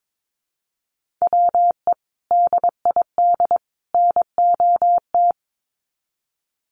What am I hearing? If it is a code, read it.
Morse code, heard as a keyed tone, about 22 wpm, 700 Hz. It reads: WE DID NOT
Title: Sets of 3 Words 22wpm